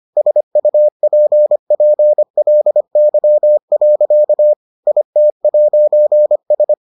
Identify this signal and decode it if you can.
Morse code, heard as a keyed tone, about 25 wpm, 600 Hz. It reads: SUPPLY. IT'S